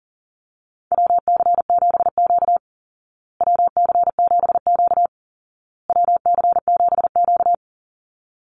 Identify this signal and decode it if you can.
Morse code, heard as a keyed tone, about 40 wpm, 700 Hz. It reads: WC7Q WC7Q WC7Q
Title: Call Signs Repeated 2x - 40wpm